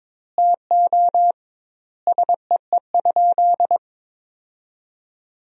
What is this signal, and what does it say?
Morse code, heard as a keyed tone, about 22 wpm, 700 Hz. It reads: TO SEE?
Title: 2x Word-Spacing - Sentences from Top 100 Words - 22wpm